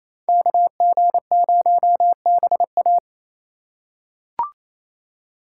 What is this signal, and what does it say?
Morse code, heard as a keyed tone, about 28 wpm, 700 Hz. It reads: KG0BA
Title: USA Call Signs - 3x Character Spacing - 28wpm